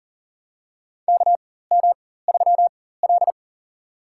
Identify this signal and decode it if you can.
Morse code, heard as a keyed tone, about 40 wpm, spaced as for 19 wpm, 700 Hz. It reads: KM3L
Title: USA Call Signs - 4 Characters - 4x Character Spacing - 40wpm